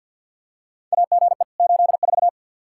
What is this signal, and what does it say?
Morse code, heard as a keyed tone, about 52 wpm, 700 Hz. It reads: AGE 84
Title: Practice QSO - Exchange 3 - More Info - 50wpm